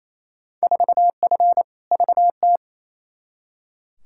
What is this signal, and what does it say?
Morse code, heard as a keyed tone, about 28 wpm, 700 Hz. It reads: 4F VT
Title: Field Day Exchanges Encore - 28wpm